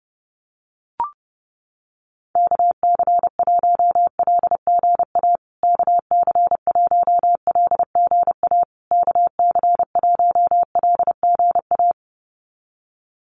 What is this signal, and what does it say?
Morse code, heard as a keyed tone, about 30 wpm, 700 Hz. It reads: KC1LGA KC1LGA KC1LGA
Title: USA Call Signs Repeated 3x 30wpm